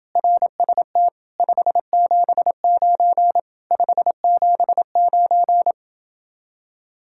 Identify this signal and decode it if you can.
Morse code, heard as a keyed tone, about 28 wpm, 700 Hz. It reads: RST 579 579